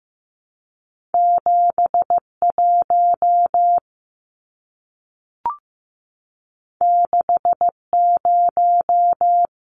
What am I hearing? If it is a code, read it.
Morse code, heard as a keyed tone, about 15 wpm, 700 Hz. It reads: 71 60
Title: Numbers - 2-Digit 15wpm